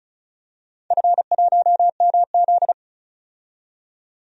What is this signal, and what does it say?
Morse code, heard as a keyed tone, about 36 wpm, 700 Hz. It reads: F1MZ
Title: DXCC-100 Call Signs - 4 Characters - 35wpm